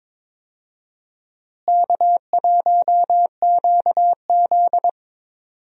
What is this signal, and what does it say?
Morse code, heard as a keyed tone, about 21 wpm, 700 Hz. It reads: K1QZ
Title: USA Call Signs - 4 Characters - Repeated 3x - 22wpm